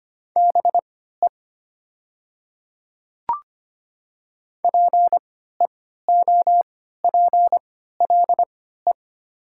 Morse code, heard as a keyed tone, about 25 wpm, spaced as for 14 wpm, 700 Hz. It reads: BE PEOPLE
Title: 3x Character Spacing - Top 100 Words - 25wpm